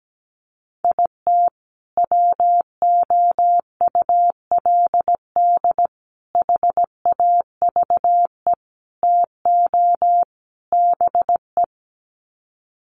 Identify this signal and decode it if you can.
Morse code, heard as a keyed tone, about 17 wpm, 700 Hz. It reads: IT WOULD HAVE TO BE
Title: Sets of 5 Words Encore 17wpm